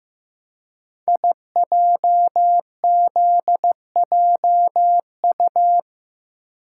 Morse code, heard as a keyed tone, about 15 wpm, 700 Hz. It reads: IJZJU
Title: Code Groups - 5 Letter:Numbers - 15wpm